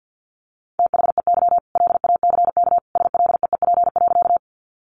Morse code, heard as a keyed tone, about 50 wpm, 700 Hz. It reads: THEY LACK SLEEP.
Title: Sentences from Top 1100 Words 50wpm